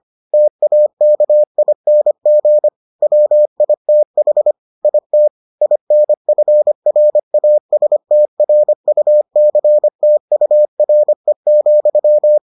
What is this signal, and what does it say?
Morse code, heard as a keyed tone, about 25 wpm, 600 Hz. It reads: TAKING WITH IT INFRASTRUCTURE,